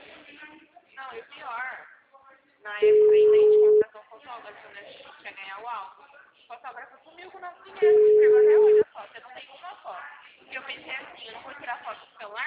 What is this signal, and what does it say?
Ringback tone (Europe, 425 Hz), with speech.